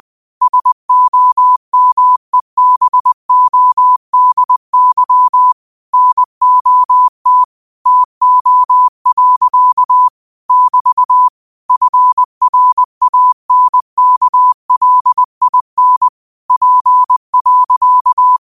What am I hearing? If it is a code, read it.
Morse code, heard as a keyed tone, about 20 wpm, 1000 Hz. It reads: SOMEBODY NOT TO. <BT> FRANKLIN P.